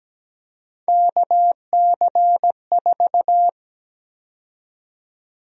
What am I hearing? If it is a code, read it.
Morse code, heard as a keyed tone, about 17 wpm, 700 Hz. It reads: KC4
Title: USA Call Sign Prefixes - 17wpm